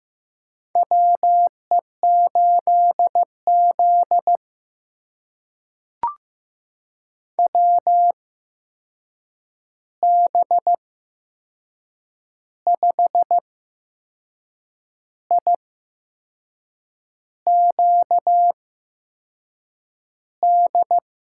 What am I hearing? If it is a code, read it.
Morse code, heard as a keyed tone, about 15 wpm, spaced as for 6 wpm, 700 Hz. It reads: WE8Z WB5IQD
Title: USA Call Signs - 8x Character Spacing - 15wpm